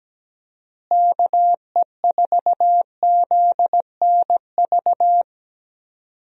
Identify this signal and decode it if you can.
Morse code, heard as a keyed tone, about 17 wpm, 700 Hz. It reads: KE4ZNV
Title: USA Call Signs - 6 Characters - Repeated 3x - 17wpm